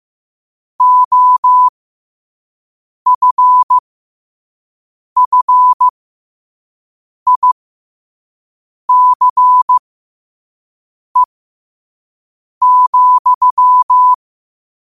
Morse code, heard as a keyed tone, about 15 wpm, spaced as for 5 wpm, 1000 Hz. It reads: OFFICE,
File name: qotdfiles / 5wpm